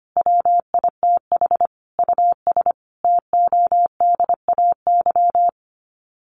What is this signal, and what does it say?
Morse code, heard as a keyed tone, about 25 wpm, 700 Hz. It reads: WITH US TODAY
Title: Sets of 5 Words Encore2 25wpm